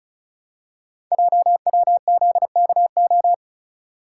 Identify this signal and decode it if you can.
Morse code, heard as a keyed tone, about 36 wpm, 700 Hz. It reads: JWZKO
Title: Code Groups - 5 Letters - 35wpm